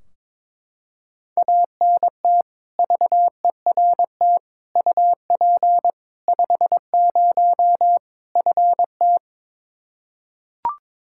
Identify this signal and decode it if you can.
Morse code, heard as a keyed tone, about 22 wpm, 700 Hz. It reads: ANT VERT UP 50 FT
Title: Practice QSO - Exchange 2 - Info - 22wpm